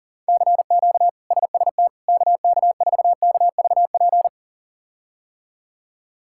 Morse code, heard as a keyed tone, about 40 wpm, 700 Hz. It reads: CQ SST KK4KVP
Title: K1USN SST QSO Practice - 40wpm